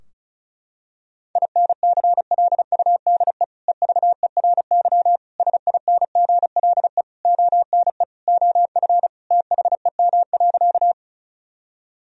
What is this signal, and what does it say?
Morse code, heard as a keyed tone, about 36 wpm, 700 Hz. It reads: INCLUDE EVERY SINGLE ONE OF THEM.
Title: Sentences from Top 1000 Words - Review Part 4 - 35wpm